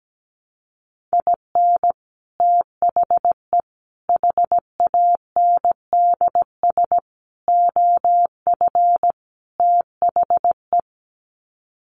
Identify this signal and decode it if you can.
Morse code, heard as a keyed tone, about 17 wpm, 700 Hz. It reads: IN THE HANDS OF THE
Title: Sets of 5 Words 17wpm